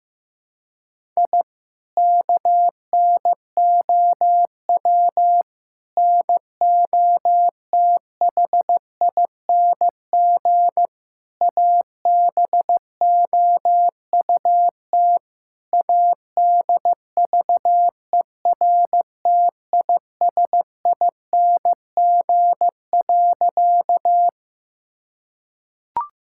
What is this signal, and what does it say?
Morse code, heard as a keyed tone, about 15 wpm, 700 Hz. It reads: I KNOW NOTHING ABOUT ADVERTISING.
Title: Sentences from Top 1000 Words - Review Part 6 - 15wpm